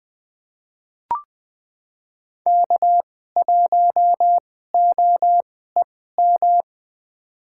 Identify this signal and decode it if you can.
Morse code, heard as a keyed tone, about 21 wpm, spaced as for 15 wpm, 700 Hz. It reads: K1OEM
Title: USA Call Signs - 5 Characters - 2x Character Spacing - 20wpm